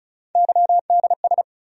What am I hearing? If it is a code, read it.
Morse code, heard as a keyed tone, about 36 wpm, 700 Hz. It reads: YDS
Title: USA Call Sign Postfixes - 3 Letters 35wpm